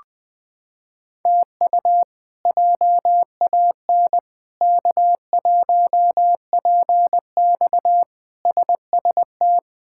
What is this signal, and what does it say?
Morse code, heard as a keyed tone, about 21 wpm, 700 Hz. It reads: TU JAN K1PX SST
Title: K1USN SST QSO Practice - 20wpm